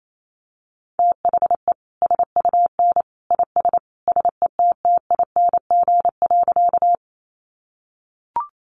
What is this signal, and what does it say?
Morse code, heard as a keyed tone, about 28 wpm, 700 Hz. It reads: THE SUN IS SETTING.
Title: Sentences from Top 1300 Words 28wpm